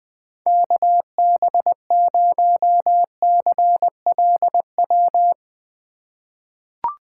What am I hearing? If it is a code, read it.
Morse code, heard as a keyed tone, about 20 wpm, 700 Hz. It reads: KB0CLW
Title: USA Call Signs - 6 Characters - 3x Character Spacing - 20wpm